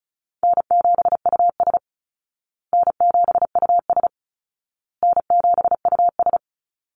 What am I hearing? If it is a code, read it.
Morse code, heard as a keyed tone, about 34 wpm, 700 Hz. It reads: N7US N7US N7US
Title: Call Signs Repeated 2x - 35wpm